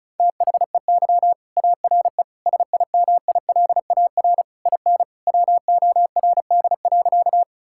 Morse code, heard as a keyed tone, about 36 wpm, 700 Hz. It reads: THEY ARE SIMILAR IN WORD.
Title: Sentences from Top 1000 Words - Review Part 5 - 35wpm